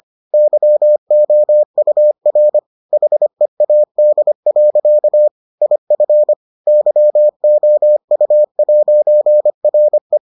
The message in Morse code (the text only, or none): YOUR HEAD. IF YOU'RE